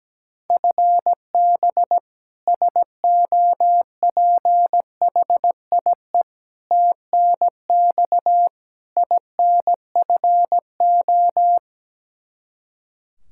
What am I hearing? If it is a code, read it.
Morse code, heard as a keyed tone, about 17 wpm, 700 Hz. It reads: FB SOPHIE TNX INFO
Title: Practice QSO - Exchange 3 - More Info - 17wpm